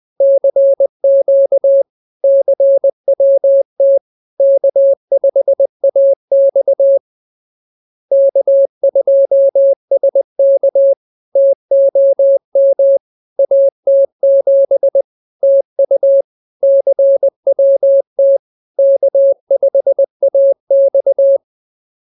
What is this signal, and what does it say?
Morse code, heard as a keyed tone, about 20 wpm, 550 Hz. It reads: CQ CWT K5AX K2SK TOM AT7 TU CWT K5AX